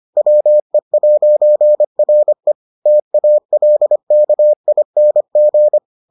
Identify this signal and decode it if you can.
Morse code, heard as a keyed tone, about 25 wpm, 600 Hz. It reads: WE'RE TALKING